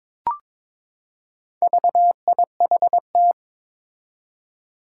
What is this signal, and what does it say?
Morse code, heard as a keyed tone, about 22 wpm, 700 Hz. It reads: VIHT
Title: Four Challenging Letter-Number Combinations 22wpm